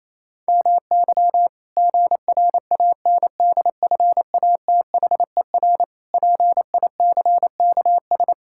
Morse code, heard as a keyed tone, about 28 wpm, 700 Hz. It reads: MY GRANDFATHER PICKS
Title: Pangram Challenge v2 - 28wpm